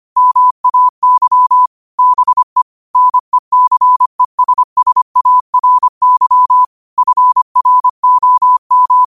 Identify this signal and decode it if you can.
Morse code, heard as a keyed tone, about 25 wpm, 1000 Hz. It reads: MAY BE NECESSARY FROM
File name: qotdfiles / 25wpm